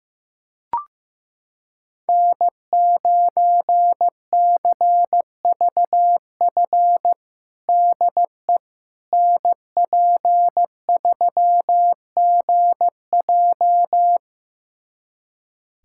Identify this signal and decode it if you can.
Morse code, heard as a keyed tone, about 15 wpm, 700 Hz. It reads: N9CVF DE NP3GJ
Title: USA Callsign DE Callsign - 5 Characters - 15wpm